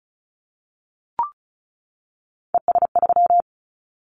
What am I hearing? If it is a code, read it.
Morse code, heard as a keyed tone, about 36 wpm, 700 Hz. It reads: ES3
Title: Callsign Prefixes for Top 100 DXCC - 35wpm